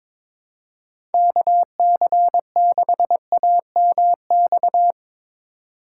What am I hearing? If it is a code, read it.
Morse code, heard as a keyed tone, about 22 wpm, 700 Hz. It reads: KC6AMX
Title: USA Call Signs - 6 Characters - 3x Character Spacing - 22wpm